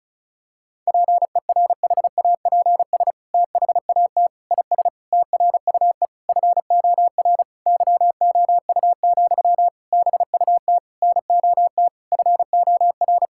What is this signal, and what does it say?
Morse code, heard as a keyed tone, about 34 wpm, 700 Hz. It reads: PERHAPS THAT IS TRUE FOR YOU, BUT NOT FOR
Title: Sentences from Top 1000 Words - Review Part 5 - 35wpm